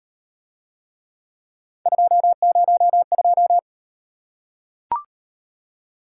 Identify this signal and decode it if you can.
Morse code, heard as a keyed tone, about 38 wpm, 700 Hz. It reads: 202